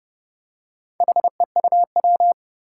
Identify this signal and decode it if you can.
Morse code, heard as a keyed tone, about 30 wpm, 700 Hz. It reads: HEUW
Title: Four Challenging Letter-Number Combinations 30wpm